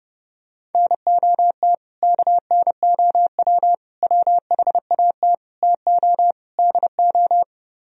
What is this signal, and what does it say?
Morse code, heard as a keyed tone, about 29 wpm, 700 Hz. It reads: NOT KNOW WHAT TO DO